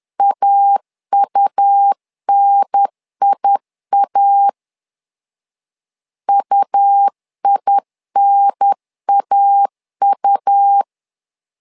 Morse code, heard as a keyed tone, about 10 wpm, 800 Hz. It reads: AUNIA UINAU
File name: LEZIONI / 2 UHDN / 2 UHDN 10wpm 1904